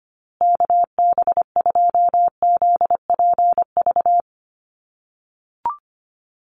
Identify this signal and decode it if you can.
Morse code, heard as a keyed tone, about 25 wpm, 700 Hz. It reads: KB2ZPV
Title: USA Call Signs Repeated 2x 25wpm